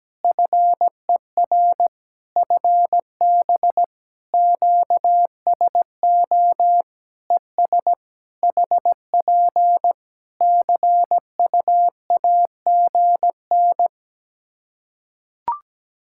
Morse code, heard as a keyed tone, about 17 wpm, 700 Hz. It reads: FER FB QSO ES HP CUAGN
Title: K1USN SST QSO Practice 17wpm